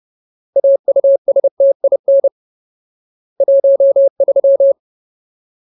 Morse code, heard as a keyed tone, about 30 wpm, 550 Hz. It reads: AUSTIN 13
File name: Ultimate Word List / 30WPM Ultimate